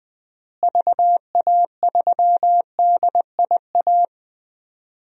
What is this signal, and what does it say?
Morse code, heard as a keyed tone, about 21 wpm, 700 Hz. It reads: VA3DIA
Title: DXCC-100 Call Signs - 6 Characters - Repeated 2x - 20wpm